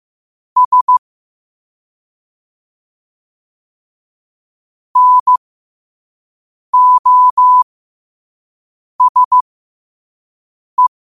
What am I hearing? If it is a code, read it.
Morse code, heard as a keyed tone, about 15 wpm, spaced as for 5 wpm, 1000 Hz. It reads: S NOSE
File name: qotdfiles / 5wpm